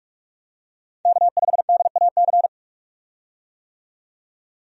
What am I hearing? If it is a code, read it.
Morse code, heard as a keyed tone, about 46 wpm, 700 Hz. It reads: K5DAC
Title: USA Call Signs - 5 Characters - 45wpm